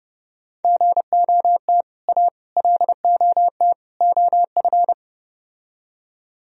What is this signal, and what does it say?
Morse code, heard as a keyed tone, about 33 wpm, 700 Hz. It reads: GOT A LOT OF